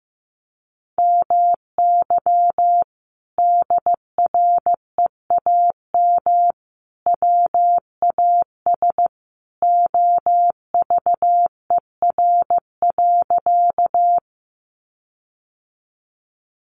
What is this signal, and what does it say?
Morse code, heard as a keyed tone, about 15 wpm, 700 Hz. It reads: MY DREAM WAS OVER.
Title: Sentences from Top 1500 Words 15wpm